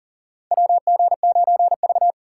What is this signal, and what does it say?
Morse code, heard as a keyed tone, about 38 wpm, 700 Hz. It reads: WG9V
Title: USA Call Signs - 4 Characters - 4x Character Spacing - 40wpm